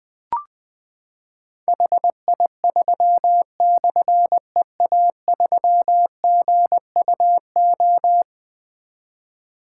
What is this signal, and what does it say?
Morse code, heard as a keyed tone, about 20 wpm, 700 Hz. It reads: HI3/EA3GUO
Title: Difficult Call Signs 20wpm